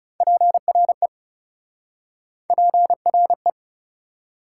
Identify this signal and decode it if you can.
Morse code, heard as a keyed tone, about 40 wpm, 700 Hz. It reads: PRE PRE